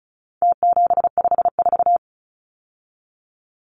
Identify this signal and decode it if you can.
Morse code, heard as a keyed tone, about 34 wpm, 700 Hz. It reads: T754
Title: Numbers - Contest Style - 4-Digit 35wpm